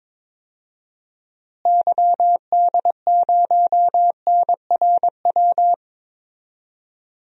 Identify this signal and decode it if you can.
Morse code, heard as a keyed tone, about 22 wpm, 700 Hz. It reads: YD0NRW
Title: DXCC-100 Call Signs - 6 Characters - Repeated 3x - 22wpm